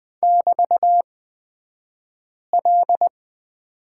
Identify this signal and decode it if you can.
Morse code, heard as a keyed tone, about 20 wpm, spaced as for 5 wpm, 700 Hz. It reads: <BT>L
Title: 242 Lesson 39d - Letter z - Two Characters 20wpm